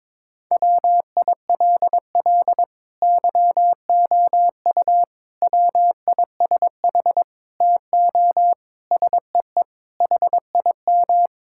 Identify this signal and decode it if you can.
Morse code, heard as a keyed tone, about 22 wpm, 700 Hz. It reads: WILL YOU WISH TO SEE HIM
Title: Sentences from Top 1600 Words 22wpm